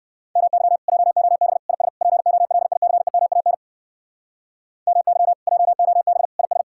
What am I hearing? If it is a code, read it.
Morse code, heard as a keyed tone, about 57 wpm, 700 Hz. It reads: MY JOB IS POLICEMAN MY JOB IS